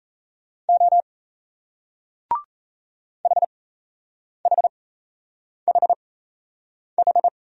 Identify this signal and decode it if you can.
Morse code, heard as a keyed tone, about 38 wpm, spaced as for 8 wpm, 700 Hz. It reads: O HHHH